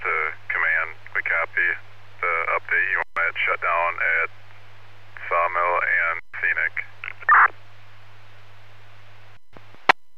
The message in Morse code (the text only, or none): none